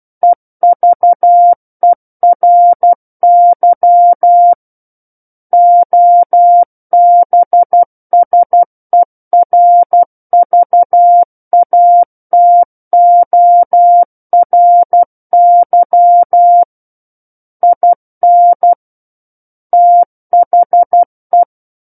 Morse code, heard as a keyed tone, about 12 wpm, 700 Hz. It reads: EVERY OBSERVATORY IN THE
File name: Book One / 03-Chapter 3 - 12 WPM